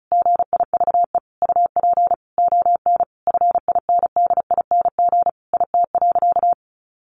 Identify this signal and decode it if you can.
Morse code, heard as a keyed tone, about 34 wpm, 700 Hz. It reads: GIVE UP ON FINDING IT.
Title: Sentences from Top 1100 Words 35wpm